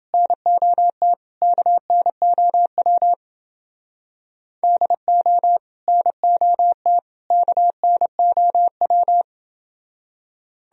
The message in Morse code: NOT KNOW DO NOT KNOW

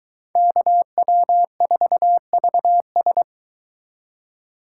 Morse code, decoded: KW4VS

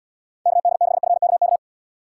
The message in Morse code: KD6FKK

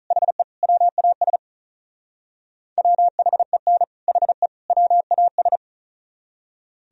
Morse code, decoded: HE WAS WHEN HE WAS

